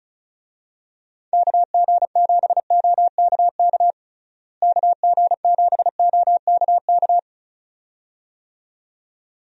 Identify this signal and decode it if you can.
Morse code, heard as a keyed tone, about 34 wpm, 700 Hz. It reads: KG7OKK KG7OKK
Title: USA Call Signs - 6 Characters - Repeated 2x - 35wpm